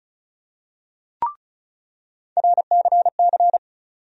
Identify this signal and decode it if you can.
Morse code, heard as a keyed tone, about 36 wpm, 700 Hz. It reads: RCC
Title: USA Call Sign Postfixes - 3 Letters 35wpm